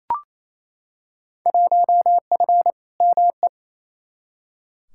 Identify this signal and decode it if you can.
Morse code, heard as a keyed tone, about 28 wpm, 700 Hz. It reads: 1F ME
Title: Field Day Exchanges Encore - 28wpm